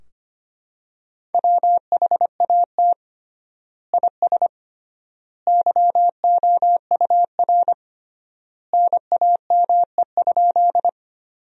Morse code, heard as a keyed tone, about 25 wpm, 700 Hz. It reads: WHAT IS YOUR NAME?
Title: 3x Word-Spacing - Sentences from Top 200 Words - 25wpm